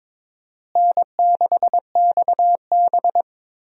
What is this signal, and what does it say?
Morse code, heard as a keyed tone, about 22 wpm, 700 Hz. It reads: N6XB
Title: USA Call Signs - 4 Characters - Repeated 2x - 22wpm